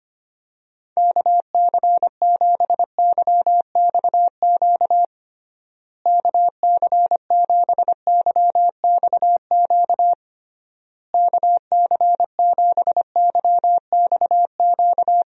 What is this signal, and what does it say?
Morse code, heard as a keyed tone, about 25 wpm, 700 Hz. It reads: KC7YXQ KC7YXQ KC7YXQ
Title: USA Call Signs - 6 Characters - Repeated 3x - 25wpm